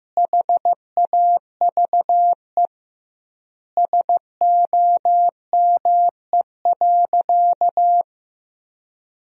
Morse code, heard as a keyed tone, about 15 wpm, 700 Hz. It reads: HAVE SOME.
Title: 2x Word-Spacing - Sentences from Top 100 Words - 15wpm